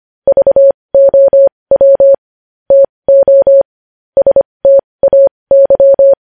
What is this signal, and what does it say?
Morse code, heard as a keyed tone, about 25 wpm, spaced as for 20 wpm, 550 Hz. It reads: VOW TO STAY
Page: news headlines in morse code at 20 wpm